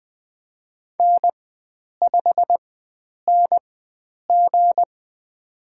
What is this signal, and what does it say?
Morse code, heard as a keyed tone, about 20 wpm, spaced as for 9 wpm, 700 Hz. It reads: N5NG
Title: USA Call Signs - 4 Characters - 4x Character Spacing - 20wpm